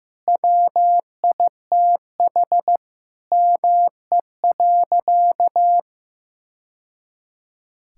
Morse code, heard as a keyed tone, about 15 wpm, 700 Hz. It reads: WITH ME.
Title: Sentences from Top 1000 Words - Review Part 6 - 15wpm